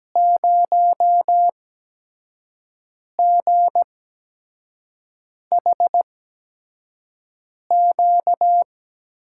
Morse code, heard as a keyed tone, about 17 wpm, spaced as for 5 wpm, 700 Hz. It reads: 0GHQ
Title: USA Call Signs - 5 Characters - 8x Character Spacing - 17wpm